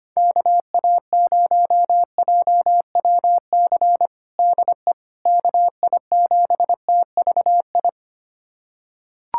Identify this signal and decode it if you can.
Morse code, heard as a keyed tone, about 25 wpm, 700 Hz. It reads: KA0JWC DE KI7TVI
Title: USA Callsign DE Callsign - 6 Characters - 25wpm